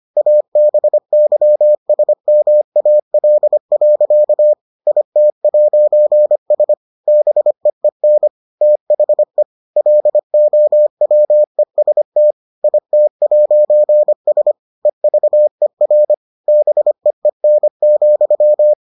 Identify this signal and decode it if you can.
Morse code, heard as a keyed tone, about 25 wpm, 600 Hz. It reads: ABYSMAL. IT'S BEEN THE LOWEST IT'S EVER BEEN,